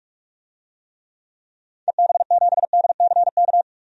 Morse code, heard as a keyed tone, about 46 wpm, 700 Hz. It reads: EB7DCK
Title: DXCC-100 Call Signs - 6 Characters - 45wpm